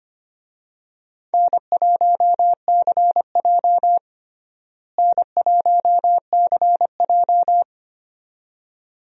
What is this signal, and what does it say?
Morse code, heard as a keyed tone, about 25 wpm, 700 Hz. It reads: N1CJ N1CJ
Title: USA Call Signs - 4 Characters - Repeated 2x - 25wpm